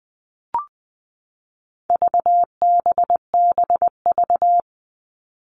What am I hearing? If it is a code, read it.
Morse code, heard as a keyed tone, about 20 wpm, 700 Hz. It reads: VBBV
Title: All Cut Numbers - 4-Digit 20wpm